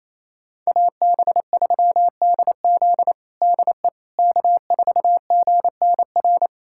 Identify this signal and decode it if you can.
Morse code, heard as a keyed tone, about 28 wpm, 700 Hz. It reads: AB3DZ DE K4GNR
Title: USA Callsign DE Callsign - 5 Characters - 28wpm